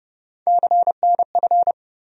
Morse code, heard as a keyed tone, about 30 wpm, 700 Hz. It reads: CNF
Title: USA Call Sign Postfixes - 3 Letters 30wpm